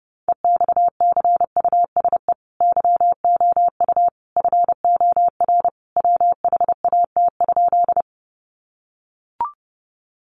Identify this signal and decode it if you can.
Morse code, heard as a keyed tone, about 30 wpm, 700 Hz. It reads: EXCUSE YOU FOR WHAT?
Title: Sentences from Top 700 Words 30wpm